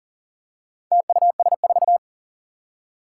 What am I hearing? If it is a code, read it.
Morse code, heard as a keyed tone, about 40 wpm, 690 Hz. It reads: TUS4
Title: Four Challenging Letter-Number Combinations 40wpm